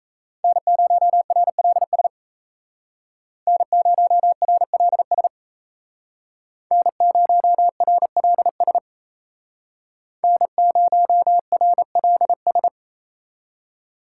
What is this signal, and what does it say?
Morse code, heard as a keyed tone, about 37 wpm, 700 Hz. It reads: N0RLS N0RLS N0RLS N0RLS